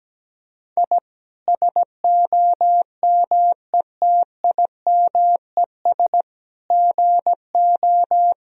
Morse code, heard as a keyed tone, about 17 wpm, 700 Hz. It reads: I SOMETIMES GO
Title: Sentences from Top 1000 Words - Review Part 3 - 17wpm